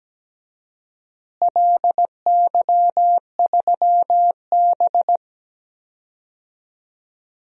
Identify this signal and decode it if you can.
Morse code, heard as a keyed tone, about 17 wpm, 700 Hz. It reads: LY3B
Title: DXCC-100 Call Signs - 4 Characters - Repeated 3x - 17wpm